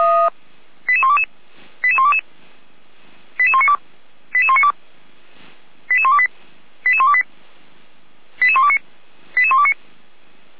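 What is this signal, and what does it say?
Five-tone selective call (ZVEI), heard as quick runs of five tones, tones of about 70 ms. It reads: 88122 88122 88192 88192 88128 88128 88129 88129
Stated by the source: ZVEI five-tone selective call 88211, received by SDR.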